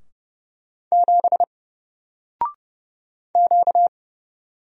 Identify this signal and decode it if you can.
Morse code, heard as a keyed tone, about 31 wpm, 700 Hz. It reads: 7 Q
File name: Single-Letter-Number-30wpm